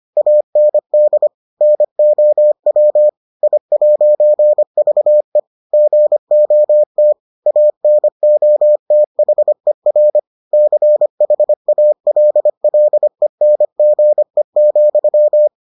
Morse code, heard as a keyed tone, about 25 wpm, 600 Hz. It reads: AND NOW I'VE GOT ANOTHER CHALLENGE,